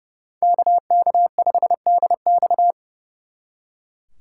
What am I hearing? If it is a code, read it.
Morse code, heard as a keyed tone, about 37 wpm, 700 Hz. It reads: KK5DX
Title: Speed Racing - USA 2x2 Call Signs 30wpm